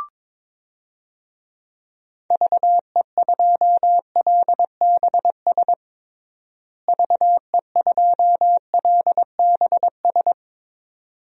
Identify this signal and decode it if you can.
Morse code, heard as a keyed tone, about 22 wpm, 700 Hz. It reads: VE2LBS VE2LBS
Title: DXCC-100 Call Signs - 6 Characters - Repeated 2x - 22wpm